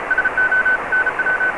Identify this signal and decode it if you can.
Morse code, heard as a keyed tone, about 34 wpm, 1500 Hz, with noise. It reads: SO NW